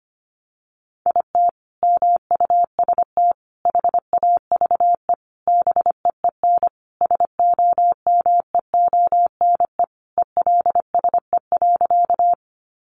Morse code, heard as a keyed tone, about 25 wpm, 700 Hz. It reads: IT MUST HAVE BEEN SOMEONE ELSE.
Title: Sentences from Top 800 Words 25wpm